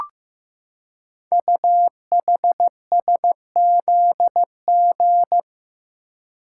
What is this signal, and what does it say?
Morse code, heard as a keyed tone, about 15 wpm, 700 Hz. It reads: UHSZG
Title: Code Groups - 5 Letters - 15wpm